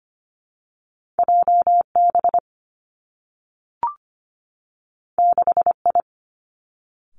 Morse code, heard as a keyed tone, about 25 wpm, 700 Hz. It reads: J B 6 I
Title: Two Challenging Letter-Number Combinations 25wpm